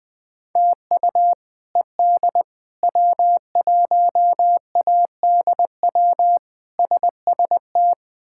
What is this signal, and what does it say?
Morse code, heard as a keyed tone, about 21 wpm, 700 Hz. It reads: TU ED W1ADW SST
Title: K1USN SST QSO Practice - 20wpm